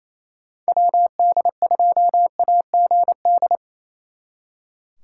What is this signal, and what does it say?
Morse code, heard as a keyed tone, about 34 wpm, 700 Hz. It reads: WD2AGD
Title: Speed Racing - USA Call Signs 28wpm